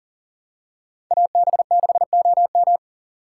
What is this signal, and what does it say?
Morse code, heard as a keyed tone, about 39 wpm, 700 Hz. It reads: AB6OM
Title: USA Call Signs - 5 Characters - 6x Character Spacing - 40wpm